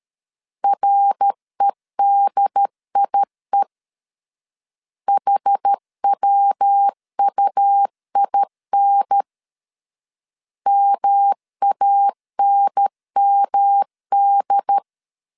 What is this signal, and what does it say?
Morse code, heard as a keyed tone, about 13 wpm, 800 Hz. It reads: REDIE HWUIN MANMD
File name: CW / DET / 1 3 1500C S B 12wpm 2828